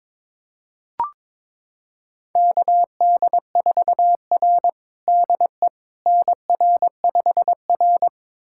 Morse code, heard as a keyed tone, about 22 wpm, 700 Hz. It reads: KD4R DE NR5R
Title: USA Callsign DE Callsign - 4 Characters - 22wpm